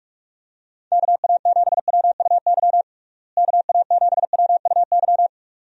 Morse code, heard as a keyed tone, about 46 wpm, 700 Hz. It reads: KA7WUY KA7WUY
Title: USA Call Signs - 6 Characters - Repeated 2x - 45wpm